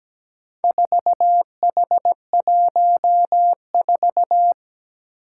Morse code, heard as a keyed tone, about 17 wpm, 700 Hz. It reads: 4H14
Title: Four Challenging Letter-Number Combinations 17wpm